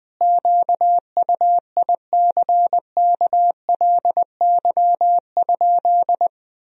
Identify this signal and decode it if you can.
Morse code, heard as a keyed tone, about 21 wpm, 700 Hz. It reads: QUICKLY?
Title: Sentences from Top 1000 Words - Review Part 3 - 20wpm